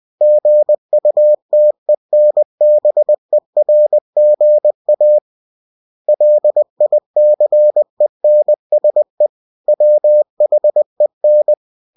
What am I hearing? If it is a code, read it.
Morse code, heard as a keyed tone, about 20 wpm, 600 Hz. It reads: GUTENBERGA LICENSE WHEN
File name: cw-books / winnie-the-pooh / winnie-20wpm-page-0114